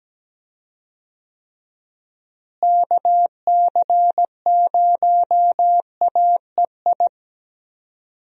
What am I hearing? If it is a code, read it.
Morse code, heard as a keyed tone, about 17 wpm, 700 Hz. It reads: KC0AEI
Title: USA Call Signs - 6 Characters - Repeated 2x - 17wpm